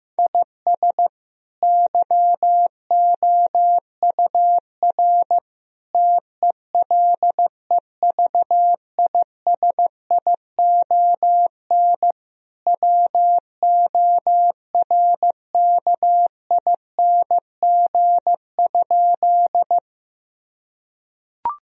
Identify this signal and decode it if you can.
Morse code, heard as a keyed tone, about 15 wpm, 700 Hz. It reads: IS YOUR TELEVISION WORKING?
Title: Sentences from Top 1000 Words - Review Part 4 - 15wpm